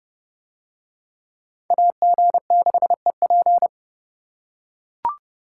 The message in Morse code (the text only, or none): AG6EP